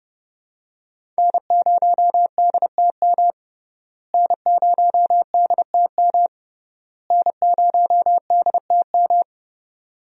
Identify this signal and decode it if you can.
Morse code, heard as a keyed tone, about 30 wpm, 700 Hz. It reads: N0DTM N0DTM N0DTM
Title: USA Call Signs - 5 Characters - Repeated 3x - 30wpm